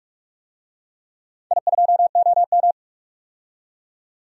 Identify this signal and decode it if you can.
Morse code, heard as a keyed tone, about 46 wpm, 700 Hz. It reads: I2OM
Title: DXCC-100 Call Signs - 4 Characters - 45wpm